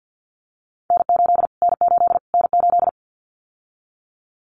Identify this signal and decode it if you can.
Morse code, heard as a keyed tone, about 49 wpm, 700 Hz. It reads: N8 N8 N8
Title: Callsign Prefixes for Top 25 DXCC - Repeated 3x - 50wpm